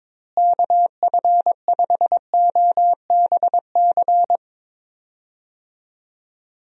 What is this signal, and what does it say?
Morse code, heard as a keyed tone, about 22 wpm, 700 Hz. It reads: KF5OBC
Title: USA Call Signs - 6 Characters - Repeated 2x - 22wpm